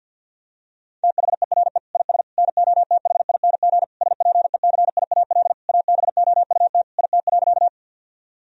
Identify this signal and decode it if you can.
Morse code, heard as a keyed tone, about 50 wpm, 700 Hz. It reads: THERE IS NOTHING SPECIAL ABOUT IT.
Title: Sentences from Top 1000 Words - Review Part 3 - 50wpm